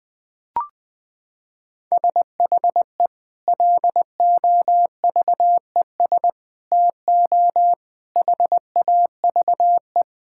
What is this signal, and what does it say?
Morse code, heard as a keyed tone, about 20 wpm, 700 Hz. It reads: SHE LOVES TO HAVE
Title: Sentences with 2-Word Phrases - 20wpm